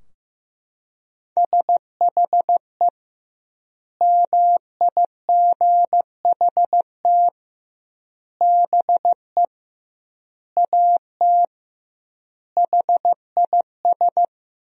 Morse code, heard as a keyed tone, about 15 wpm, 700 Hz. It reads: SHE MIGHT BE AT HIS
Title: 2x Word-Spacing - Sentences from Top 300 Words - 15wpm